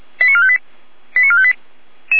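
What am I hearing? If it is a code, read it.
Five-tone selective call (ZVEI), heard as quick runs of five tones, tones of about 70 ms. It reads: 89458 89458
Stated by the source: ZVEI five-tone selective call received on SDR.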